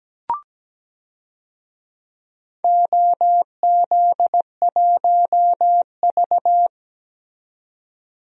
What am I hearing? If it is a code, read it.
Morse code, heard as a keyed tone, about 17 wpm, 700 Hz. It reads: OZ1V
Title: DXCC-100 Call Signs - 4 Characters - 17wpm